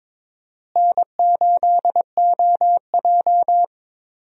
Morse code, heard as a keyed tone, about 22 wpm, 700 Hz. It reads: N8OJ
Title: USA Call Signs - 4 Characters - Repeated 2x - 22wpm